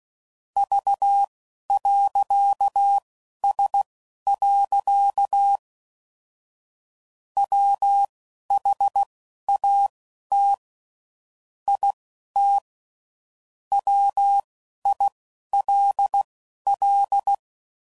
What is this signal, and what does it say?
Morse code, heard as a keyed tone, about 16 wpm, spaced as for 11 wpm, 800 Hz. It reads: V.S. WHAT IT WILL
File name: CW Code Groups @10WPM/Technical Subjects